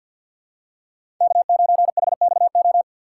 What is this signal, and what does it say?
Morse code, heard as a keyed tone, about 51 wpm, 700 Hz. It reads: K9HXO
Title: USA Call Signs - 5 Characters - 6x Character Spacing - 50wpm